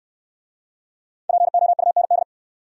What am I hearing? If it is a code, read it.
Morse code, heard as a keyed tone, about 61 wpm, 700 Hz. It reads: POUND